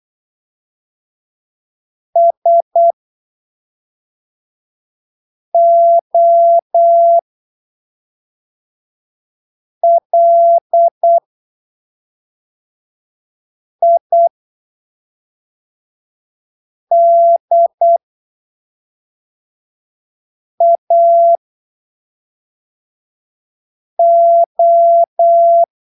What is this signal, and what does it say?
Morse code, heard as a keyed tone, about 8 wpm, spaced as for 3 wpm, 680 Hz. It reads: SOLIDAO
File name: arquivos / texto / texto 23 08ppm